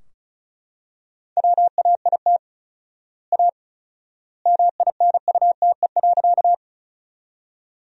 Morse code, heard as a keyed tone, about 36 wpm, 700 Hz. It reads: WAIT A MINUTE.
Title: 4x Word-Spacing - Sentences from Top 500 Words - 35wpm